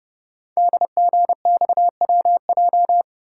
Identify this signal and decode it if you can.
Morse code, heard as a keyed tone, about 30 wpm, 700 Hz. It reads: DGXWJ
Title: Code Groups - 5 Letters - 30wpm